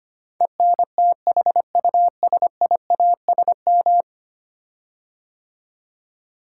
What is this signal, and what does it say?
Morse code, heard as a keyed tone, about 25 wpm, 700 Hz. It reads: ENTHUSIASM
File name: 10 Letter Words - First 3 Letters Given - 25wpm